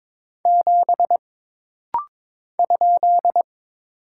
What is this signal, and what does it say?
Morse code, heard as a keyed tone, about 22 wpm, 700 Hz. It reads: 7 ?
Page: This rapid-fire timing is geared to push your ICR (Instant Character Recognition) skills.